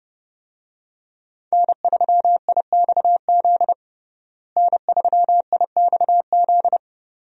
Morse code, heard as a keyed tone, about 32 wpm, 700 Hz. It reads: N3IXZ N3IXZ
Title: USA Call Signs - 5 Characters - Repeated 2x - 30wpm